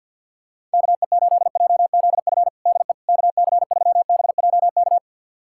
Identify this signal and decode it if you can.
Morse code, heard as a keyed tone, about 52 wpm, 700 Hz. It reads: KE8JZF DE KC3BJK
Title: USA Callsign DE Callsign - 6 Characters - 50wpm